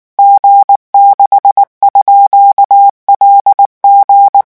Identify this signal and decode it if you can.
Morse code, heard as a keyed tone, about 19 wpm, 800 Hz. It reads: G6_LG
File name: morse mp3 / revision12-vitesse 18 mots